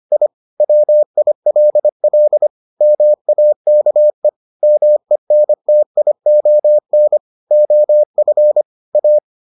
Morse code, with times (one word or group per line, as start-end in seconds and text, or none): 0.1-0.3 I
0.6-2.5 WILL
2.8-4.3 MAKE
4.6-7.2 MENTION
7.5-8.6 OF
9.0-9.2 A